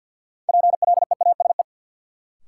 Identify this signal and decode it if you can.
Morse code, heard as a keyed tone, about 53 wpm, 700 Hz. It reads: PLEASE
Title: Sentences from Top 1500 Words - New Word Drill Alt - 50wpm